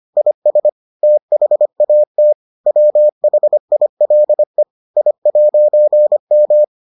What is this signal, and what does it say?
Morse code, heard as a keyed tone, about 25 wpm, 600 Hz. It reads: IS THAT WHILE I'M